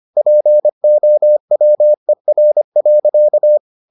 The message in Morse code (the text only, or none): POWER.